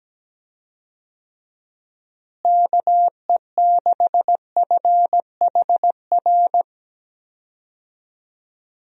Morse code, heard as a keyed tone, about 17 wpm, 700 Hz. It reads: KE6FHR
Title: USA Call Signs - 6 Characters - Repeated 2x - 17wpm